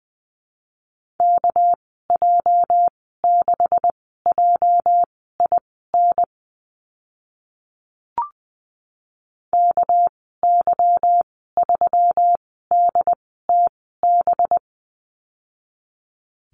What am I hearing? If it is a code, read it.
Morse code, heard as a keyed tone, about 20 wpm, spaced as for 15 wpm, 700 Hz. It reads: KJ6JIN KY3DTB
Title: 245 Lesson 39g - Letter z - Call Signs 20wpm